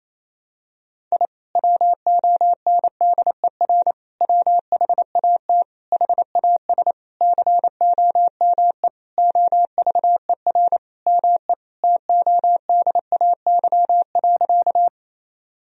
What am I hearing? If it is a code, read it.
Morse code, heard as a keyed tone, about 28 wpm, 700 Hz. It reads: I WONDER WHAT HAS COME OVER ME TODAY.
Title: Sentences from Top 1000 Words - Review Part 4 - 28wpm